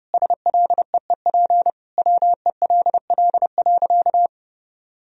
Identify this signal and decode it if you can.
Morse code, heard as a keyed tone, about 30 wpm, 700 Hz. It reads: SLEEP WELL.
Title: Sentences from Top 500 Words and Abbreviations 30wpm